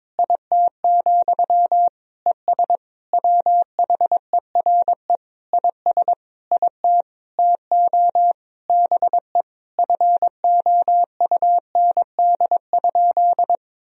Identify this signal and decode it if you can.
Morse code, heard as a keyed tone, about 22 wpm, 700 Hz. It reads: IT, ES WHERE IS IT TO BE FOUND?
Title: Sentences from Top 300 Words and Abbreviations 22wpm